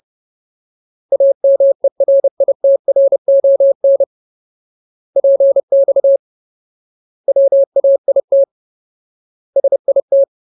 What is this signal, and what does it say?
Morse code, heard as a keyed tone, about 30 wpm, 550 Hz. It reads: AMERITRON PX WAIT SIT
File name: Ultimate Word List / 30WPM Ultimate